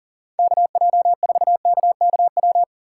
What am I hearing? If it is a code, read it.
Morse code, heard as a keyed tone, about 40 wpm, 700 Hz. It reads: KJ4KKW
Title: USA Call Signs - 6 Characters - 6x Character Spacing - 40wpm